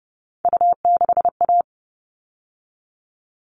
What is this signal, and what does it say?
Morse code, heard as a keyed tone, about 29 wpm, 700 Hz. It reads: U6A
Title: All Cut Numbers - 3-Digit 30wpm